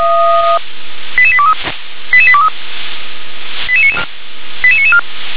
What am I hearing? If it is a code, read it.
Five-tone selective call (ZVEI), heard as quick runs of five tones, tones of about 70 ms. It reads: 88823 88823 88994